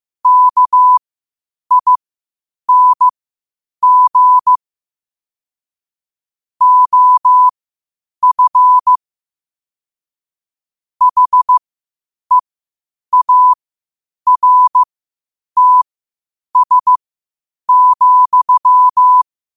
Morse code, heard as a keyed tone, about 15 wpm, spaced as for 8 wpm, 1000 Hz. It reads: KING OF HEARTS,
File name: qotdfiles / 8wpm